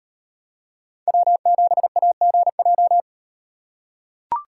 Morse code, heard as a keyed tone, about 38 wpm, 700 Hz. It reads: W7AGJ